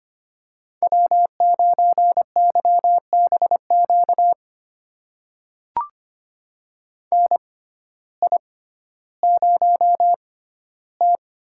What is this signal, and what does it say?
Morse code, heard as a keyed tone, about 25 wpm, spaced as for 14 wpm, 700 Hz. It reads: W9YBQ NI0T
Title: USA Call Signs - 6x Character Spacing - 25wpm